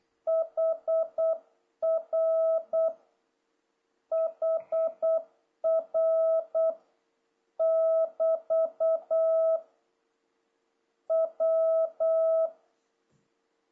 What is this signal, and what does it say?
Morse code, heard as a keyed tone, about 8 wpm, 650 Hz. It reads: HR HR <BT> W
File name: jh8cbh / 2026cw